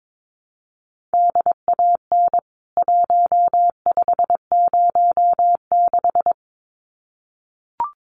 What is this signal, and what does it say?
Morse code, heard as a keyed tone, about 22 wpm, 700 Hz. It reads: DAN 1506
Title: CWT Exchange 22wpm